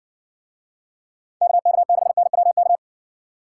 Morse code, heard as a keyed tone, about 57 wpm, 700 Hz. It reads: KK6NWK